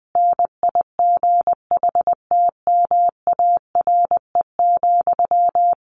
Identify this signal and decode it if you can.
Morse code, heard as a keyed tone, about 20 wpm, 700 Hz. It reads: NIGHTMARE,
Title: A Sound Of Thunder 20wpm